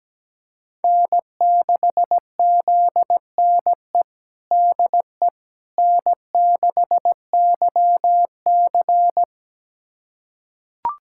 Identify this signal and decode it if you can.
Morse code, heard as a keyed tone, about 17 wpm, 700 Hz. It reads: N6ZNE DE N6YC
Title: USA Callsign DE Callsign 17wpm